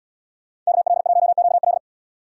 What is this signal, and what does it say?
Morse code, heard as a keyed tone, about 76 wpm, 700 Hz. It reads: KL1QL